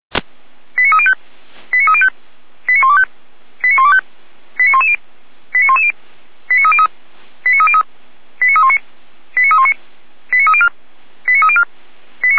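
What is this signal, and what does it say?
Five-tone selective call (ZVEI), heard as quick runs of five tones, tones of about 70 ms. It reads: 89305 89305 89125 89125 89110 89110 89303 89303 89319 89319 89304 89304
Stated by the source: ZVEI five-tone selective call, code 89265, received by SDR, feuerwehr-orlamuende fire brigade audio server.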